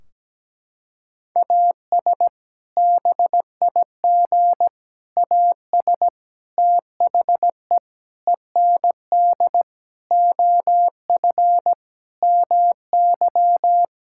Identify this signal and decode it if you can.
Morse code, heard as a keyed tone, about 17 wpm, 700 Hz. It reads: AS BIG AS THE END OF MY
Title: Sentences from Top 1600 Words 17wpm